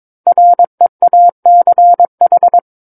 Morse code, heard as a keyed tone, about 22 wpm, 700 Hz. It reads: REACH
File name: Book One / 13-Chapter 13 - 22 WPM